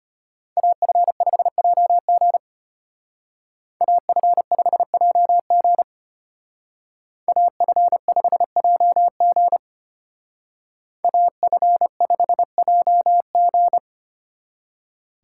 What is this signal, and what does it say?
Morse code, heard as a keyed tone, about 33 wpm, 700 Hz. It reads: AF5JG AF5JG AF5JG AF5JG